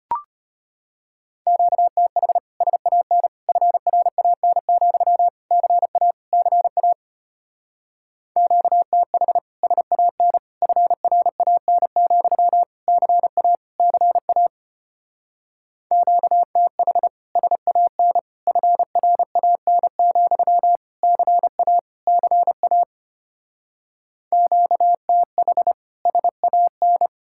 Morse code, heard as a keyed tone, about 33 wpm, 700 Hz. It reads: QTH SAN FRAN, CA CA QTH SAN FRAN, CA CA QTH SAN FRAN, CA CA QTH SAN